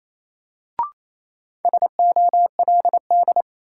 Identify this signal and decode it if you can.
Morse code, heard as a keyed tone, about 28 wpm, 700 Hz. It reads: SOLD
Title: Sentences from Top 1400 Words - New Word Drill Alt - 28wpm